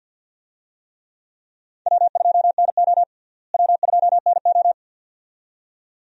Morse code, heard as a keyed tone, about 52 wpm, 700 Hz. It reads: W2NO W2NO
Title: USA Call Signs - 4 Characters - Repeated 2x - 50wpm